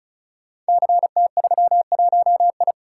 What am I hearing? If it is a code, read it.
Morse code, heard as a keyed tone, about 36 wpm, 690 Hz. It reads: CT31I
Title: Code Groups - 5 Letter:Numbers - 35wpm